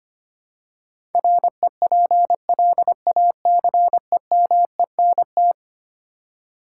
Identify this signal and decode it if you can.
Morse code, heard as a keyed tone, about 25 wpm, 700 Hz. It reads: REPLACEMENT
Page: morse-code-ninja-practice-25wpm